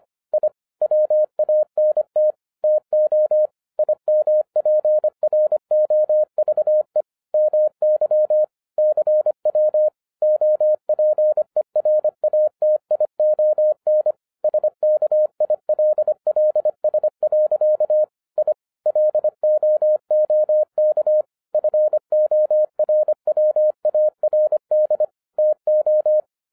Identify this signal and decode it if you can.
Morse code, heard as a keyed tone, about 25 wpm, 610 Hz. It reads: I WANT TO IMPROVE MY CW OPERATION SKILLS. I LOOK FORWARD TO